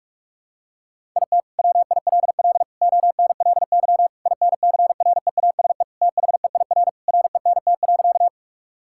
Morse code, heard as a keyed tone, about 41 wpm, 700 Hz. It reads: IT WILL ONLY INCREASE THEIR RENT.
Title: Sentences from Top 1000 Words - Review Part 5 - 45wpm